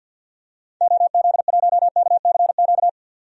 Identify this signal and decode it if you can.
Morse code, heard as a keyed tone, about 50 wpm, 700 Hz. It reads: OZ1KCQ
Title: DXCC-100 Call Signs - 6 Characters - 50wpm